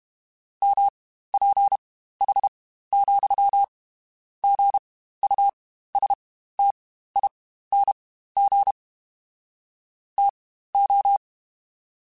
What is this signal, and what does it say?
Morse code, heard as a keyed tone, about 32 wpm, spaced as for 15 wpm, 800 Hz. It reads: MPH, GUSTING TO
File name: Spaced Characters / Spaced Characters-mp3s / Weather 4 21 sp @30